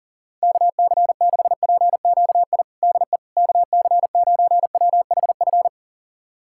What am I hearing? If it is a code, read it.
Morse code, heard as a keyed tone, about 40 wpm, 700 Hz. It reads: KC6PQI DE KC9WHF
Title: USA Callsign DE Callsign - 6 Characters - 40wpm